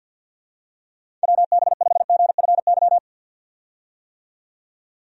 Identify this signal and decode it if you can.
Morse code, heard as a keyed tone, about 50 wpm, 700 Hz. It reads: WB5GFY
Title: USA Call Signs - 6 Characters - 50wpm